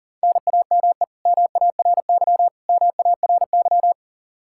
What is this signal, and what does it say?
Morse code, heard as a keyed tone, about 40 wpm, 700 Hz. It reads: NAME MARY MARY
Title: K1USN SST QSO Practice 40wpm